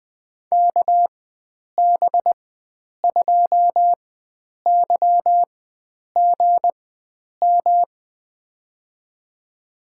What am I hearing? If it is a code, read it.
Morse code, heard as a keyed tone, about 20 wpm, spaced as for 9 wpm, 700 Hz. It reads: KB2YGM
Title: USA Call Signs - 6 Characters - 4x Character Spacing - 20wpm